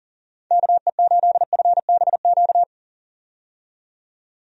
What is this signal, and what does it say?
Morse code, heard as a keyed tone, about 40 wpm, 700 Hz. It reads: KE8FBQ
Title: USA Call Signs - 6 Characters - 40wpm